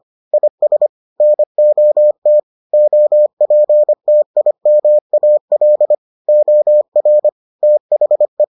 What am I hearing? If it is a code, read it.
Morse code, heard as a keyed tone, about 25 wpm, 600 Hz. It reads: IS NOT OPTIMAL OR THE